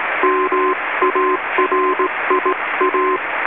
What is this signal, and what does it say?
Morse code, heard as a keyed tone, about 17 wpm, 1040 Hz, with radio noise. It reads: MARIA
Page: All of these recordings were made by the author onboard the New Zealand inter island rail ferry ARAHURA / ZMBS in November 1993 when I was Radio Officer.